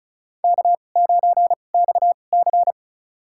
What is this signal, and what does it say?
Morse code, heard as a keyed tone, about 34 wpm, spaced as for 25 wpm, 700 Hz. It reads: K9XC
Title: USA Call Signs - 4 Characters - 2x Character Spacing - 35wpm